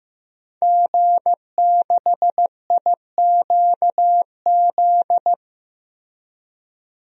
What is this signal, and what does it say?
Morse code, heard as a keyed tone, about 16 wpm, 700 Hz. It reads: G6IQZ
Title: DXCC-100 Call Signs - 5 Characters - Repeated 2x - 15wpm